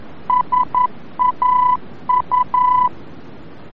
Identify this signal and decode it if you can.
Morse code, heard as a keyed tone, about 11 wpm, 1000 Hz. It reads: SAU
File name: france / mx / SAU-VOR-F